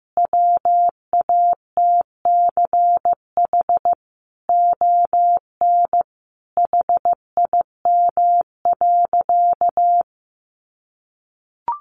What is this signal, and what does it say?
Morse code, heard as a keyed tone, about 15 wpm, 700 Hz. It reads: WATCH ON HIM.
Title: Sentences from Top 1300 Words 15wpm